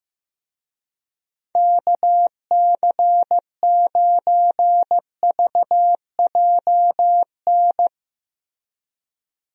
Morse code, heard as a keyed tone, about 15 wpm, 700 Hz. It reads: KC9VJN
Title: USA Call Signs - 6 Characters - Repeated 2x - 15wpm